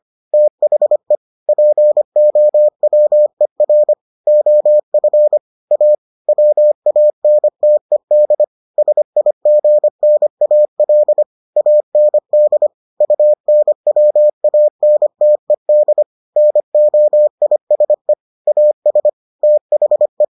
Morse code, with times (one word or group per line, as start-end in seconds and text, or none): 0.3-1.1 THE
1.5-3.9 POWER
4.3-5.4 OF
5.7-6.0 A
6.3-8.4 WANTED
8.8-11.2 SIGNAL
11.6-12.7 AND
13.0-16.0 UNWANTED
16.4-18.1 NOISE
18.5-19.1 AS
19.4-20.2 THE